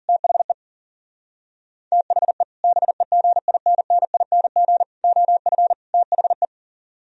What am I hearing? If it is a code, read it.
Morse code, heard as a keyed tone, about 52 wpm, 700 Hz. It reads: THE THE BEGINNING OF THE